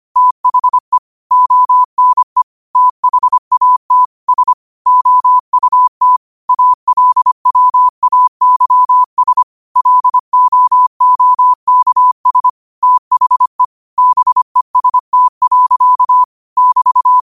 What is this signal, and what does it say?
Morse code, heard as a keyed tone, about 25 wpm, 1000 Hz. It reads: THE ONE THAT S OUT ALWAYS LOOKS THE BEST. <BT>
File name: qotdfiles / 25wpm